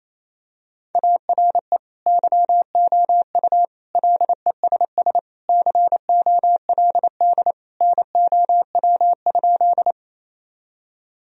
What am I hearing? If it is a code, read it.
Morse code, heard as a keyed tone, about 28 wpm, 700 Hz. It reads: ARE YOU LESS COLD NOW?
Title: Sentences from Top 1000 Words - Review Part 2 - 28wpm